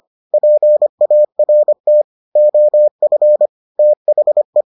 Morse code, heard as a keyed tone, about 25 wpm, 600 Hz. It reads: PART OF THE